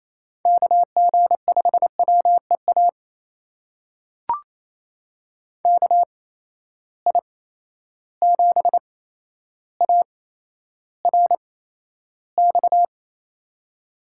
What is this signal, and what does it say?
Morse code, heard as a keyed tone, about 28 wpm, spaced as for 12 wpm, 700 Hz. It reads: KG5WEA KI7ARX
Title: USA Call Signs - 6 Characters - 8x Character Spacing - 28wpm